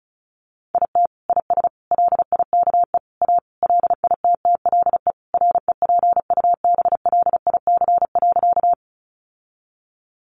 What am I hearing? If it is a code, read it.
Morse code, heard as a keyed tone, about 34 wpm, 700 Hz. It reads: IT IS LIKE A LITTLE REPUBLIC.
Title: Sentences from Top 1500 Words 35wpm